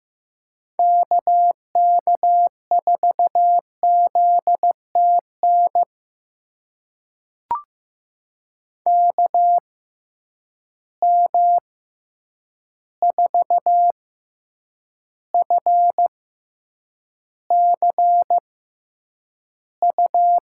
Morse code, heard as a keyed tone, about 15 wpm, spaced as for 8 wpm, 700 Hz. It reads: KK4ZTN KM4FCU
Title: USA Call Signs - 6 Characters - 6x Character Spacing - 15wpm